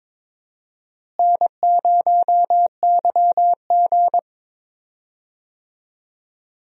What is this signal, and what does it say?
Morse code, heard as a keyed tone, about 22 wpm, 700 Hz. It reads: N0YG
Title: USA Call Signs - 4 Characters - Repeated 2x - 22wpm